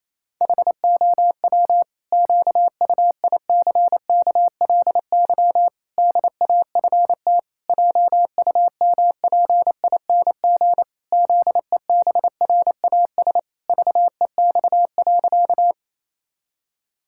Morse code, heard as a keyed tone, about 27 wpm, 700 Hz. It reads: HOW QUICKLY DAFT JUMPING ZEBRAS VEX.
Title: Pangram Challenge v2 - 28wpm